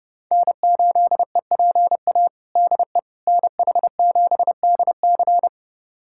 Morse code, heard as a keyed tone, about 30 wpm, 700 Hz. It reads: N8EPA DE NH7DC
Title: USA Callsign DE Callsign - 5 Characters - 30wpm